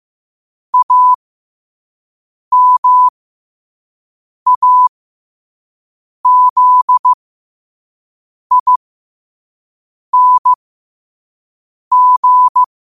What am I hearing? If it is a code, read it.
Morse code, heard as a keyed tone, about 15 wpm, spaced as for 5 wpm, 1000 Hz. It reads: AMAZING